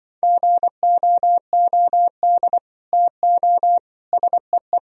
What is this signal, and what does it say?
Morse code, heard as a keyed tone, about 25 wpm, 700 Hz. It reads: GOOD TO SEE